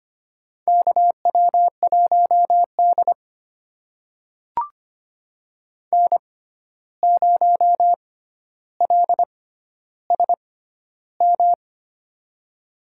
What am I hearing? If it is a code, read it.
Morse code, heard as a keyed tone, about 25 wpm, spaced as for 12 wpm, 700 Hz. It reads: KW1D N0LSM
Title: USA Call Signs - 6x Character Spacing - 25wpm